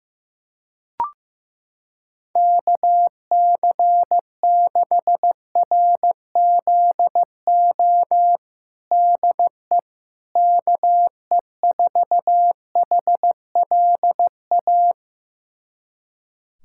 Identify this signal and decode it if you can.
Morse code, heard as a keyed tone, about 16 wpm, 700 Hz. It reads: KC6RZO DE KE4HLA
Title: USA Callsign DE Callsign - 6 Characters - 15wpm